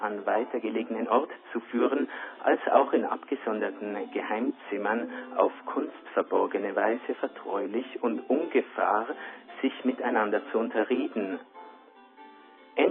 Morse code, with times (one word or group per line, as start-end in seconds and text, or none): none